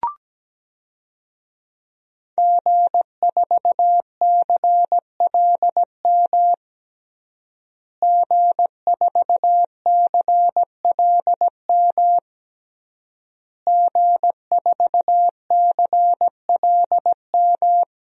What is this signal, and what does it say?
Morse code, heard as a keyed tone, about 17 wpm, 700 Hz. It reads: G4CLM G4CLM G4CLM
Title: DXCC-100 Call Signs - 5 Characters - Repeated 3x - 17wpm